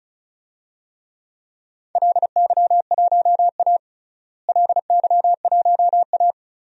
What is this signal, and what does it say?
Morse code, heard as a keyed tone, about 36 wpm, 700 Hz. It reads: LY1A LY1A
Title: DXCC-100 Call Signs - 4 Characters - Repeated 3x - 35wpm